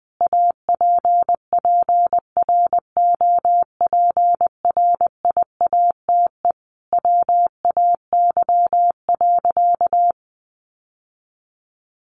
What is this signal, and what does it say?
Morse code, heard as a keyed tone, about 20 wpm, 700 Hz. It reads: APPROPRIATE WAY.
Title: Sentences from Top 900 Words 20wpm